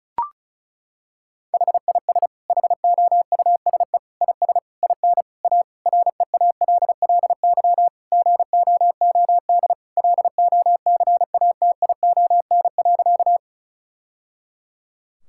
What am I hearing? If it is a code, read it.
Morse code, heard as a keyed tone, about 36 wpm, 690 Hz. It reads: HIS HOUSE IS IN A REALLY GOOD LOCATION.
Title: Sentences from Top 1000 Words - Review Part 6 - 35wpm